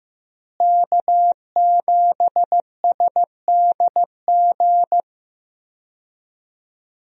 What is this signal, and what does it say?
Morse code, heard as a keyed tone, about 15 wpm, 700 Hz. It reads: K7SDG
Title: USA Call Signs - 5 Characters - Repeated 3x - 15wpm